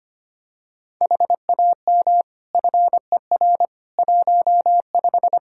Morse code, heard as a keyed tone, about 25 wpm, 700 Hz. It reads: HAM FER 15
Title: Practice QSO - Exchange 3 - More Info - 25wpm